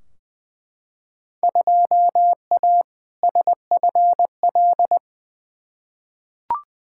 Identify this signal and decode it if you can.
Morse code, heard as a keyed tone, about 20 wpm, 700 Hz. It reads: 2A SFL
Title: Field Day Exchanges Encore - 20wpm